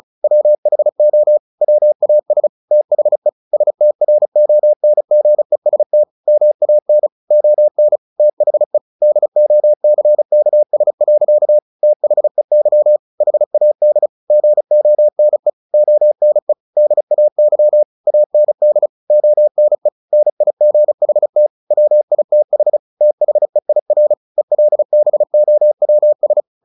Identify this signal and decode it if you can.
Morse code, heard as a keyed tone, about 34 wpm, 600 Hz. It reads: WHO WAS THE STRONGEST MAN ON THE DOCKS. THEY HAD GONE ONE DAY AND ONE NIGHT WITH THEIR ELBOWS
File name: the old man and the sea / the old man and the sea 1 35wpm